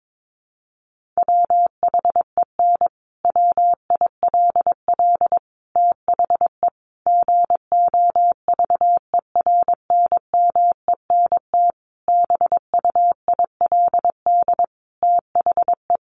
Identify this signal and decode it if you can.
Morse code, heard as a keyed tone, about 22 wpm, 700 Hz. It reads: WHEN WILL THE GOVERNMENT BUILD THE
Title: Sentences from Top 1400 Words 22wpm